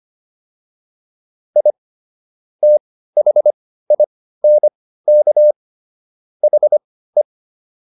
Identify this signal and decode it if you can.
Morse code, heard as a keyed tone, about 25 wpm, spaced as for 15 wpm, 600 Hz. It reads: I THINK HE